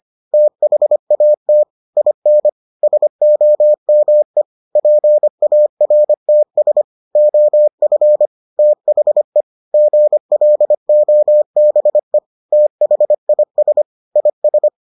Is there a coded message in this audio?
Yes: Morse code from 0.3 to 14.7 s, heard as a keyed tone, about 25 wpm, 600 Hz. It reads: THAT IN SOME PARTS OF THE GLOBE THIS IS